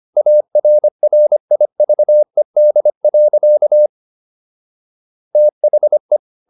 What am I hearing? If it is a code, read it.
Morse code, heard as a keyed tone, about 25 wpm, 600 Hz. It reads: ARRIVED. THE